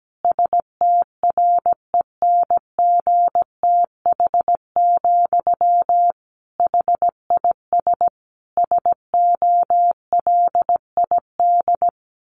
Morse code, heard as a keyed tone, about 17 wpm, 700 Hz. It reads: STRENGTH, HIS SOLID
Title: Sentences from Top 1500 Words 17wpm